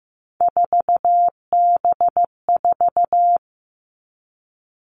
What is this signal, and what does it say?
Morse code, heard as a keyed tone, about 15 wpm, 700 Hz. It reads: 4B4
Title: Three Challenging Letter-Number Combinations 15wpm